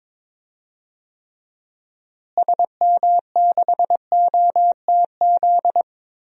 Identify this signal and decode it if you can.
Morse code, heard as a keyed tone, about 21 wpm, 700 Hz. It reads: SM6OTZ
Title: DXCC-100 Call Signs - 6 Characters - Repeated 2x - 22wpm